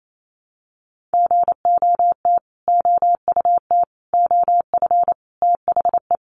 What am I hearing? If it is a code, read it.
Morse code, heard as a keyed tone, about 28 wpm, 700 Hz. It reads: GOT OUT OF THE
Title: Sets of 5 Words Encore 28wpm